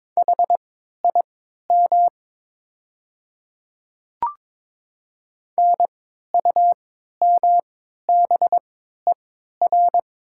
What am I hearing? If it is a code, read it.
Morse code, heard as a keyed tone, about 22 wpm, spaced as for 12 wpm, 700 Hz. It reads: HIM NUMBER
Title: 3x Character Spacing - Top 100 Words - 22wpm